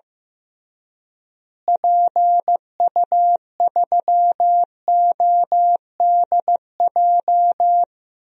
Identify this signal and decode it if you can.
Morse code, heard as a keyed tone, about 15 wpm, 700 Hz. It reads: PU3ODJ
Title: DXCC-100 Call Signs - 6 Characters - Repeated 3x - 15wpm